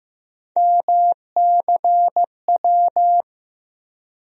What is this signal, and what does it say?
Morse code, heard as a keyed tone, about 15 wpm, 700 Hz. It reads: MCW
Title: USA Call Sign Postfixes - 3 Letters 15wpm